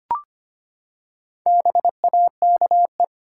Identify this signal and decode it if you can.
Morse code, heard as a keyed tone, about 25 wpm, 700 Hz. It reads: BAKE
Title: Four Letter Words - 25wpm